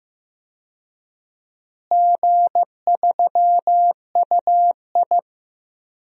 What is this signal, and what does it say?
Morse code, heard as a keyed tone, about 15 wpm, 700 Hz. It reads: G3UI
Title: DXCC-100 Call Signs - 4 Characters - Repeated 2x - 15wpm